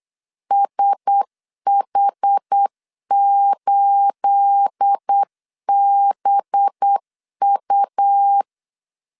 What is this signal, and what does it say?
Morse code, heard as a keyed tone, about 8 wpm, 800 Hz. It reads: SH8BU
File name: LEZIONI / 9 28 8wpm 3512